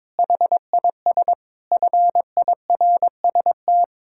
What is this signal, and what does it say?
Morse code, heard as a keyed tone, about 22 wpm, 700 Hz. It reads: HIS FIRST
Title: Pangram Challenge v2 - 22wpm